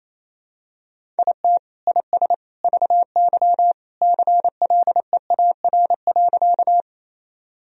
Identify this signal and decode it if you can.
Morse code, heard as a keyed tone, about 28 wpm, 700 Hz. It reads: IT IS VY CLEAR.
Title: Sentences from Top 500 Words and Abbreviations 28wpm